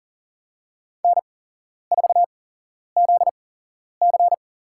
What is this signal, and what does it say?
Morse code, heard as a keyed tone, about 40 wpm, spaced as for 11 wpm, 700 Hz. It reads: N4ZC
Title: USA Call Signs - 4 Characters - 8x Character Spacing - 40wpm